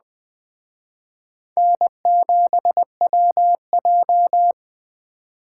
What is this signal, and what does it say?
Morse code, heard as a keyed tone, about 20 wpm, 700 Hz. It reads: N7WJ
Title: Speed Racing - USA Call Signs 17wpm